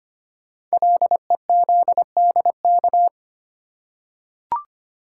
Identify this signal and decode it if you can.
Morse code, heard as a keyed tone, about 25 wpm, 700 Hz. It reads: LEZDK
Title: Code Groups - 5 Letters - 25wpm